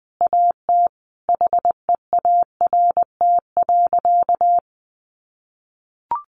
Morse code, heard as a keyed tone, about 19 wpm, 700 Hz. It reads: AT HEART.
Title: Sentences from Top 1500 Words 20wpm